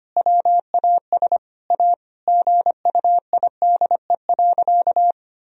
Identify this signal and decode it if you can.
Morse code, heard as a keyed tone, about 25 wpm, 700 Hz. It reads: WAS A GUIDE.
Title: Sentences from Top 1000 Words - Review Part 3 - 25wpm